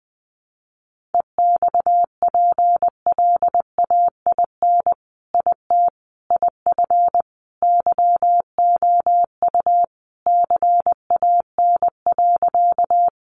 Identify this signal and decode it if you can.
Morse code, heard as a keyed tone, about 20 wpm, 700 Hz. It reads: EXPLAIN IT IF YOU CAN.
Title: Sentences from Top 1100 Words 20wpm